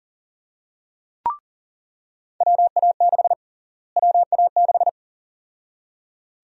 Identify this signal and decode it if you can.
Morse code, heard as a keyed tone, about 37 wpm, 700 Hz. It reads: WA6 WA6
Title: USA Call Sign Prefixes - Repeated 2x - 40wpm